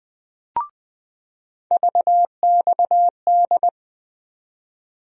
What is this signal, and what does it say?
Morse code, heard as a keyed tone, about 20 wpm, 700 Hz. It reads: VXD
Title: USA Call Sign Postfixes - 3 Letters 20wpm